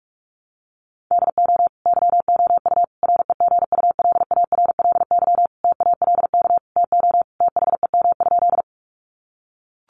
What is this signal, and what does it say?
Morse code, heard as a keyed tone, about 44 wpm, 700 Hz. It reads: DO YOU REGULARLY TALK TO THEM?
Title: Sentences from Top 1400 Words 45wpm